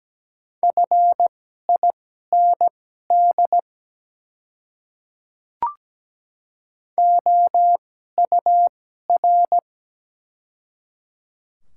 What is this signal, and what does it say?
Morse code, heard as a keyed tone, about 17 wpm, spaced as for 12 wpm, 700 Hz. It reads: FIND OUR
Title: 2x Character Spacing - Top 300 Words - 17wpm